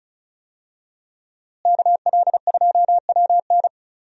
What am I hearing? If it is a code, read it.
Morse code, heard as a keyed tone, about 34 wpm, 700 Hz. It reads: KL2WN
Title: USA Call Signs - 5 Characters - Repeated 3x - 35wpm